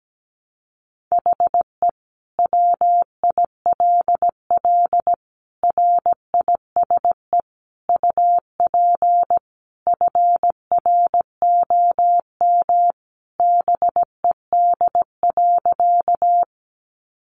Morse code, heard as a keyed tone, about 17 wpm, 700 Hz. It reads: HE WILL RISE UP FROM BED.
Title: Sentences from Top 900 Words 17wpm